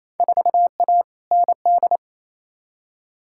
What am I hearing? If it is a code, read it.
Morse code, heard as a keyed tone, about 28 wpm, 700 Hz. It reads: 4A ND
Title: Field Day Exchanges Encore - 28wpm